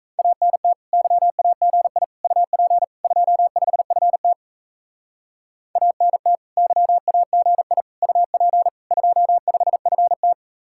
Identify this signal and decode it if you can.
Morse code, heard as a keyed tone, about 34 wpm, 700 Hz. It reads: ANT YAGI UP 25FT ANT YAGI UP 25FT
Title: Speed-Racing Common QSO Phrases - 28wpm